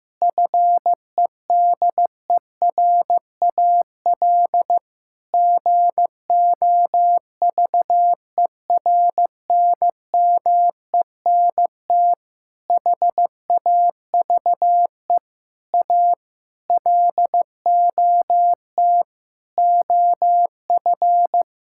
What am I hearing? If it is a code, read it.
Morse code, heard as a keyed tone, about 16 wpm, 700 Hz. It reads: FEDERAL GOVERNMENT HAVE A LOT OF
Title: Sentences from Top 1000 Words - Review Part 5 - 15wpm